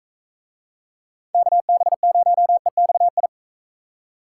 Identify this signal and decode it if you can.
Morse code, heard as a keyed tone, about 39 wpm, 700 Hz. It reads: KB0EXI